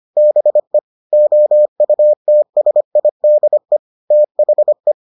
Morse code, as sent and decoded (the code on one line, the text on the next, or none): -... / --- ..- - ... .. -.. / - .... .
BE OUTSIDE THE